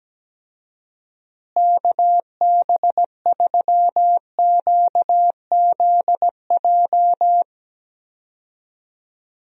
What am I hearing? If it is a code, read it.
Morse code, heard as a keyed tone, about 17 wpm, 700 Hz. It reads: KB3QZJ
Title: USA Call Signs - 6 Characters - Repeated 3x - 17wpm